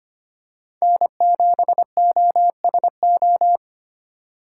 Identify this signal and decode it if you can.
Morse code, heard as a keyed tone, about 25 wpm, 700 Hz. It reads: N7OSO
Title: USA Call Signs - 5 Characters - 4x Character Spacing - 25wpm